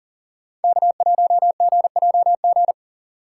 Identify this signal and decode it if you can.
Morse code, heard as a keyed tone, about 39 wpm, 700 Hz. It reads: K1GJG
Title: USA Call Signs - 2x Character Spacing - 40wpm